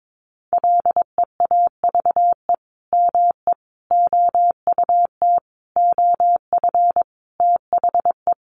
Morse code, heard as a keyed tone, about 22 wpm, 700 Hz. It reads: LEAVE ME OUT OF THE
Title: Sentences from Top 400 Words 22wpm